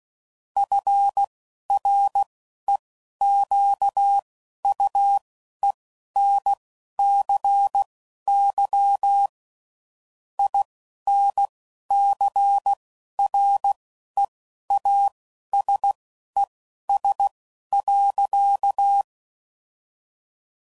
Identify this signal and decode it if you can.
Morse code, heard as a keyed tone, about 16 wpm, spaced as for 12 wpm, 800 Hz. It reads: FREQUENCY INCREASES.
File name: CW Code Groups @10WPM / Technical Subjects / XC